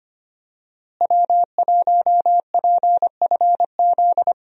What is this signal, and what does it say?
Morse code, heard as a keyed tone, about 25 wpm, 700 Hz. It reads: W1PFZ
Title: USA Call Signs - 5 Characters - Repeated 3x - 25wpm